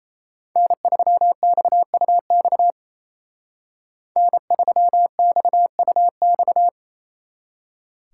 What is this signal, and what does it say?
Morse code, heard as a keyed tone, about 33 wpm, 700 Hz. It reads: N3XUX N3XUX
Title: Speed Racing - USA 1x3 Call Signs 28wpm